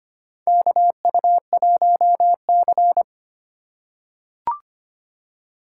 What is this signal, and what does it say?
Morse code, heard as a keyed tone, about 25 wpm, 700 Hz. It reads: KU1C
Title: USA Call Signs - 4 Characters - 4x Character Spacing - 25wpm